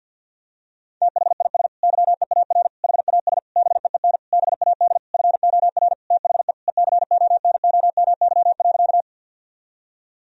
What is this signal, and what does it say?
Morse code, heard as a keyed tone, about 46 wpm, 700 Hz. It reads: THIS YEAR HAS BEEN BAD FOR THE ECONOMY.
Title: Sentences from Top 1000 Words - Review Part 3 - 50wpm